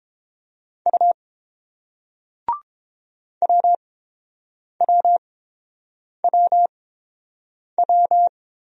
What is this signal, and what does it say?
Morse code, heard as a keyed tone, about 31 wpm, spaced as for 7 wpm, 700 Hz. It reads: U WWWW